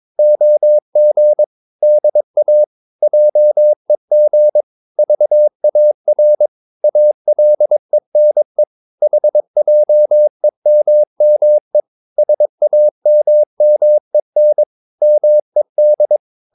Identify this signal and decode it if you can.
Morse code, heard as a keyed tone, about 22 wpm, 600 Hz. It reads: OG DA JEG VAR ALENE HJEMME SAMMEN MED
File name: Ferie QRS 22wpm